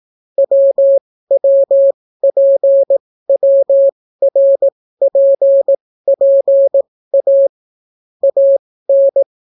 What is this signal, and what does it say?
Morse code, heard as a keyed tone, about 18 wpm, spaced as for 14 wpm, 550 Hz. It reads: WWPWRPPA AN